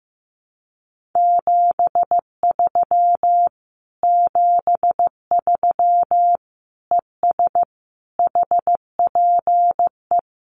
Morse code, heard as a keyed tone, about 15 wpm, 700 Hz. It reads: 73 73 ES HPE
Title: Common QSO Phrases Encore 15wpm